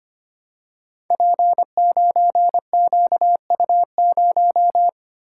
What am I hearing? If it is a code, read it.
Morse code, heard as a keyed tone, about 25 wpm, 700 Hz. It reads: P9QU0
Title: Code Groups - 5 Letter:Numbers - 25wpm